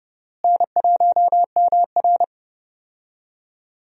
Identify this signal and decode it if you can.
Morse code, heard as a keyed tone, about 30 wpm, 700 Hz. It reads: N1MR
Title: USA Call Signs - 4 Characters - Repeated 2x - 30wpm